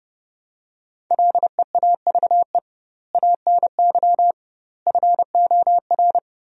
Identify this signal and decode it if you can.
Morse code, heard as a keyed tone, about 30 wpm, 700 Hz. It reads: LEAVE ANY FOR
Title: 2x Word-Spacing - Sentences from Top 400 Words - 30wpm